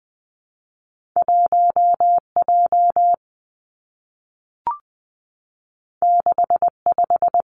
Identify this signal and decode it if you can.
Morse code, heard as a keyed tone, about 19 wpm, 700 Hz. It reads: 1 J 6 5
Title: Two Challenging Letter-Number Combinations 20wpm